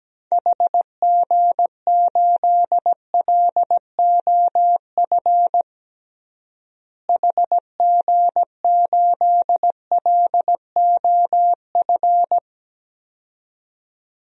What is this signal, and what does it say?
Morse code, heard as a keyed tone, about 17 wpm, 700 Hz. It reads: HG8LOF HG8LOF
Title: DXCC-100 Call Signs - 6 Characters - Repeated 2x - 17wpm